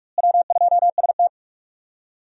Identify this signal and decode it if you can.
Morse code, heard as a keyed tone, about 43 wpm, 700 Hz. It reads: W2ST
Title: USA Call Signs - 4 Characters - Repeated 2x - 45wpm